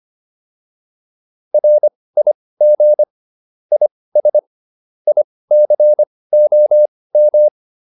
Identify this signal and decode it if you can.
Morse code, heard as a keyed tone, about 25 wpm, spaced as for 18 wpm, 600 Hz. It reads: RIG IS ICOM